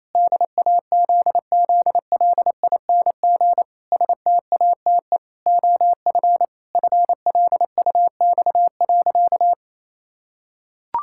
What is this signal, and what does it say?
Morse code, heard as a keyed tone, about 28 wpm, 700 Hz. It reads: DAZZLING STATE OF FLUX.
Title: Pangram Challenge v2 - 28wpm